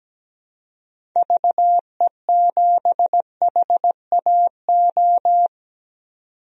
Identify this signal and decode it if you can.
Morse code, heard as a keyed tone, about 17 wpm, 700 Hz. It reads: VE7HAO
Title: DXCC-100 Call Signs - 6 Characters - 17wpm